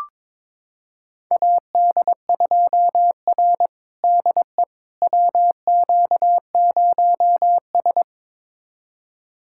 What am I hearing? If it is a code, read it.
Morse code, heard as a keyed tone, about 22 wpm, 700 Hz. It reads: AD2R DE WQ0S
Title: USA Callsign DE Callsign - 4 Characters - 22wpm